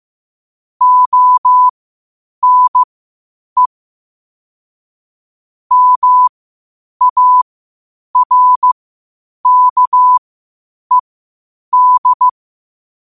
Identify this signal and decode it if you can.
Morse code, heard as a keyed tone, about 15 wpm, spaced as for 8 wpm, 1000 Hz. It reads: ONE MARKED
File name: qotdfiles / 8wpm